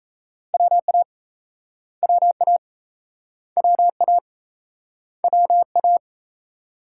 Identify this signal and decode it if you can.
Morse code, heard as a keyed tone, about 35 wpm, 700 Hz. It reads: WA WA WA WA
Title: Speed-Racing Multi-letter Phonemes - 28wpm